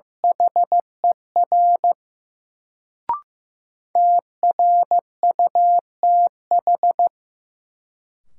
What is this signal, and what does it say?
Morse code, heard as a keyed tone, about 15 wpm, 700 Hz. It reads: HER TRUTH
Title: Sentences from Top 800 Words - New Word Drill - 15wpm